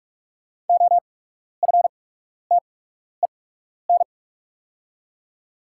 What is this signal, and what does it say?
Morse code, heard as a keyed tone, about 45 wpm, spaced as for 12 wpm, 690 Hz. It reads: OFTEN
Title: 8x Character Spacing - Top 500 Words - 45wpm